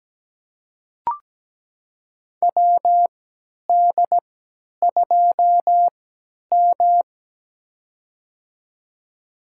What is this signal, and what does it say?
Morse code, heard as a keyed tone, about 17 wpm, spaced as for 10 wpm, 700 Hz. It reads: WD2M
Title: USA Call Signs - 4 Characters - 3x Character Spacing - 17wpm